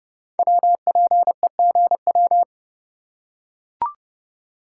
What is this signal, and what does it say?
Morse code, heard as a keyed tone, about 30 wpm, 700 Hz. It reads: WPEGW
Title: Code Groups - 5 Letter:Numbers - 30wpm